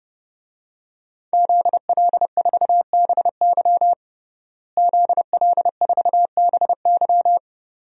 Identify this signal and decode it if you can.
Morse code, heard as a keyed tone, about 30 wpm, 700 Hz. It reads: ZL4BY ZL4BY
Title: DXCC-100 Call Signs - 5 Characters - Repeated 2x - 30wpm